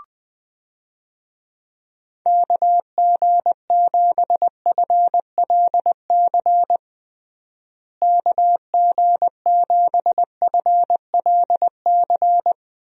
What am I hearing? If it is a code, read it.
Morse code, heard as a keyed tone, about 20 wpm, 700 Hz. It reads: KG7FLC KG7FLC
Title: USA Call Signs - 6 Characters - Repeated 2x - 20wpm